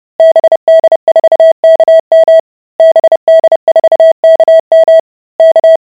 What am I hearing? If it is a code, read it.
Morse code, heard as a keyed tone, about 30 wpm, 650 Hz. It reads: BD4KM BD4KM K